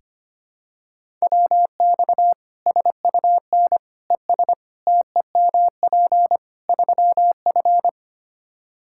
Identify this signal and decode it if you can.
Morse code, heard as a keyed tone, about 25 wpm, 700 Hz. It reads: WX SUN ES TEMP 3F
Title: K1USN SST QSO Practice 25wpm